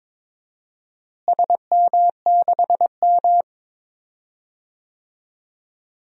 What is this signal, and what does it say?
Morse code, heard as a keyed tone, about 22 wpm, 700 Hz. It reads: SM6M
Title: DXCC-100 Call Signs - 4 Characters - Repeated 2x - 22wpm